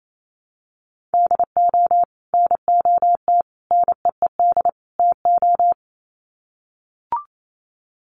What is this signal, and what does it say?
Morse code, heard as a keyed tone, about 28 wpm, 700 Hz. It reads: DO NOT NEED TO
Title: Sets of 4 Words 28wpm